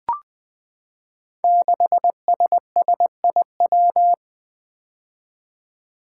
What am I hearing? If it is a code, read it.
Morse code, heard as a keyed tone, about 20 wpm, 700 Hz. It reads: 6SSIW
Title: Code Groups - 5 Letter:Numbers - 20wpm